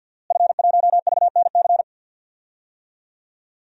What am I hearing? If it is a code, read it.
Morse code, heard as a keyed tone, about 52 wpm, 700 Hz. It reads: F1VNC
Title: DXCC-100 Call Signs - 5 Characters - Repeated 3x - 50wpm